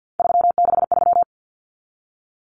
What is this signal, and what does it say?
Morse code, heard as a keyed tone, about 50 wpm, 700 Hz. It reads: 363
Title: Common Cut Numbers - 3-Digit 50wpm